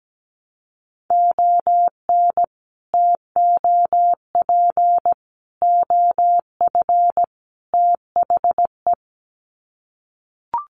Morse code, heard as a keyed tone, about 17 wpm, 700 Hz. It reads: ON TOP OF THE
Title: Sets of 4 Words 17wpm